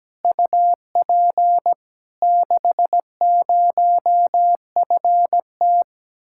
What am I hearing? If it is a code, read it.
Morse code, heard as a keyed tone, about 17 wpm, 700 Hz. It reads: UP 60FT
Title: Speed-Racing Common QSO Phrases - 17wpm